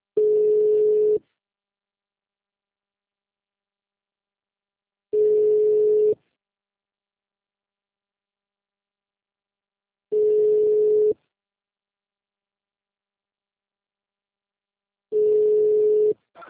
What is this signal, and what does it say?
Ringback tone (Europe, 425 Hz).